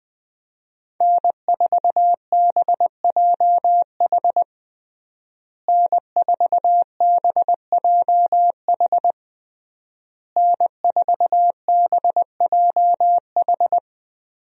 Morse code, heard as a keyed tone, about 21 wpm, 700 Hz. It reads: N4BJH N4BJH N4BJH
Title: USA Call Signs - 5 Characters - Repeated 3x - 20wpm